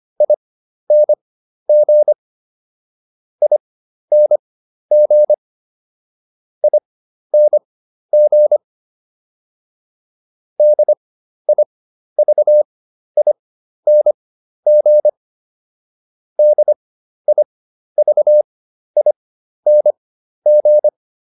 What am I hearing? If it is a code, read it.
Morse code, heard as a keyed tone, about 25 wpm, spaced as for 12 wpm, 600 Hz. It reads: ING ING ING DIVING DIVING